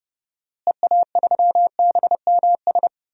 Morse code, heard as a keyed tone, about 32 wpm, 700 Hz. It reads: EA3BMS
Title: DXCC-100 Call Signs - 6 Characters - Repeated 2x - 30wpm